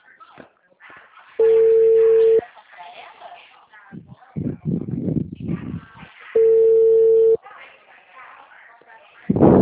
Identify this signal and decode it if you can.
Ringback tone (China, 450 Hz), with speech.
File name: out- -2009- - - .37167